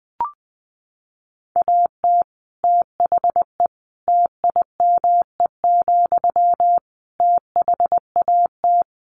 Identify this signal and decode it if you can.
Morse code, heard as a keyed tone, about 20 wpm, 700 Hz. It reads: AT THE TIME, THAT
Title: Sentences from Top 1400 Words 20wpm